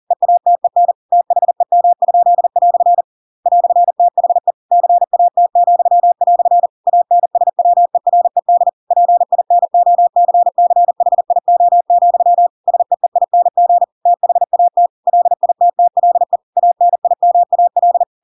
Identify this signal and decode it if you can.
Morse code, heard as a keyed tone, about 40 wpm, 700 Hz. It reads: EATEN THEM?" "THE CAT," ANSWERED PINOCCHIO, SEEING THAT LITTLE ANIMAL
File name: Pinocchio / Pinocchio ch7 40 40 ik1hge